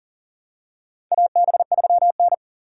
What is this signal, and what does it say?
Morse code, heard as a keyed tone, about 37 wpm, 700 Hz. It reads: AB3N
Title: USA Call Signs - 4 Characters - 3x Character Spacing - 40wpm